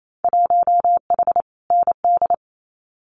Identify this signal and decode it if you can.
Morse code, heard as a keyed tone, about 28 wpm, 700 Hz. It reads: 1H ND
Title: Winter Field Day Exchanges 28wpm